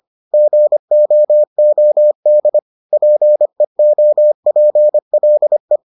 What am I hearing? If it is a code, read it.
Morse code, heard as a keyed tone, about 25 wpm, 600 Hz. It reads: GOOD PEOPLE